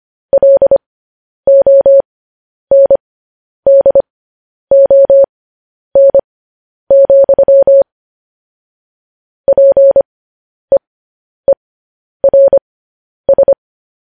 Morse code, heard as a keyed tone, about 25 wpm, spaced as for 10 wpm, 550 Hz. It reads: LONDON, PEERS
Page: News Headlines in Morse Code at 10 WPM